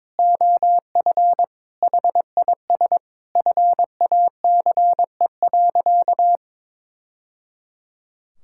Morse code, heard as a keyed tone, about 22 wpm, 700 Hz. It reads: OF HIS FACE.
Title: Sentences from Top 1000 Words - Review Part 5 - 22wpm